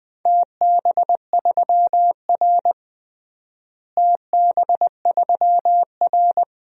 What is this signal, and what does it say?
Morse code, heard as a keyed tone, about 21 wpm, 700 Hz. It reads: TB3R TB3R
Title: DXCC-100 Call Signs - 4 Characters - Repeated 3x - 20wpm